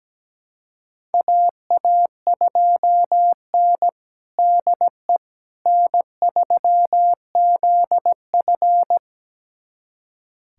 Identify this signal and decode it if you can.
Morse code, heard as a keyed tone, about 17 wpm, 700 Hz. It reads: AA2N DE N3ZF
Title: USA Callsign DE Callsign - 4 Characters - 17wpm